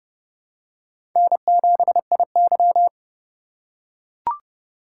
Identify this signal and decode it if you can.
Morse code, heard as a keyed tone, about 32 wpm, 700 Hz. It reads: N7IY
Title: USA Call Signs - 4 Characters - 3x Character Spacing - 30wpm